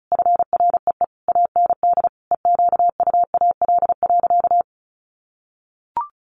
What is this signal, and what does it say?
Morse code, heard as a keyed tone, about 34 wpm, 700 Hz. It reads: FREE AND EQUAL.
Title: Sentences from Top 1500 Words 35wpm